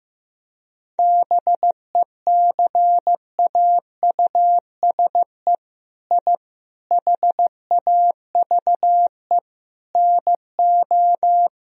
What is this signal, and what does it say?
Morse code, heard as a keyed tone, about 15 wpm, 700 Hz. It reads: BECAUSE I HAVE NO
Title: Sentences from Top 1000 Words - Review Part 3 - 15wpm